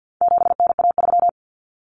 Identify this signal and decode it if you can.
Morse code, heard as a keyed tone, about 50 wpm, 700 Hz. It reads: 7NA3
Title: Common Cut Numbers - 4-Digit 50wpm